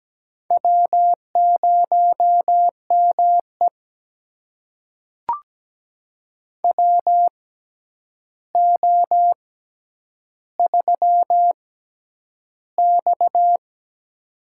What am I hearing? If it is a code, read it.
Morse code, heard as a keyed tone, about 17 wpm, spaced as for 9 wpm, 700 Hz. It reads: W0ME WO3X
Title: USA Call Signs - 4 Characters - 6x Character Spacing - 17wpm